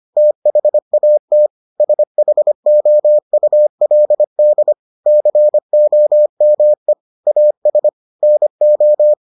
Morse code, heard as a keyed tone, about 25 wpm, 600 Hz. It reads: THAT SHOULD COME AS NO